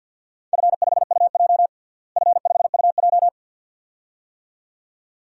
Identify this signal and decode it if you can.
Morse code, heard as a keyed tone, about 52 wpm, 700 Hz. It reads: F5UJ F5UJ
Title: DXCC-100 Call Signs - 4 Characters - Repeated 2x - 50wpm